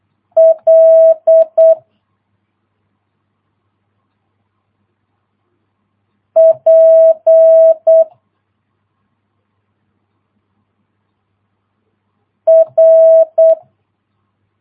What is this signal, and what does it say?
Morse code, heard as a keyed tone, about 8 wpm, 650 Hz. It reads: L P R